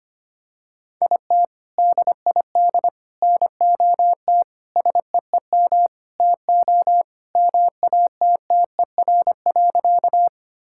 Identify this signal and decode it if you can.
Morse code, heard as a keyed tone, about 25 wpm, 700 Hz. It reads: IT DID NOT SEEM TO MATTER.
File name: practice / 25wpm / Sentences with 4-Word Phrases - 25wpm